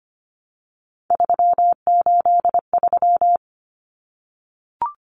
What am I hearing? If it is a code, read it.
Morse code, heard as a keyed tone, about 25 wpm, 700 Hz. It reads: 383
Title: Numbers - 3-Digit 25wpm